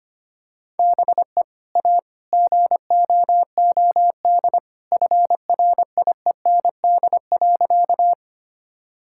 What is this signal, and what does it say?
Morse code, heard as a keyed tone, about 25 wpm, 700 Hz. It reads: BE A GOOD FRIEND.
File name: practice / 25wpm / Sentences with 2-Word Phrases - 25wpm